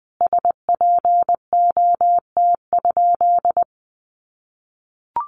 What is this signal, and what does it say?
Morse code, heard as a keyed tone, about 20 wpm, 700 Hz. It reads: SPOT?
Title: Sentences from Top 1100 Words 20wpm